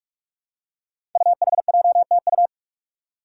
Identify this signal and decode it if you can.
Morse code, heard as a keyed tone, about 44 wpm, 700 Hz. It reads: UHJTU
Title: Five Challenging Letter-Number Combinations 45wpm